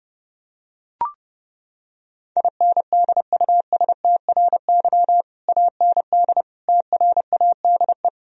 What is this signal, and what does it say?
Morse code, heard as a keyed tone, about 30 wpm, 700 Hz. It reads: INDUSTRY AND TRADE
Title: Sentences from Top 1000 Words - Review Part 4 - 30wpm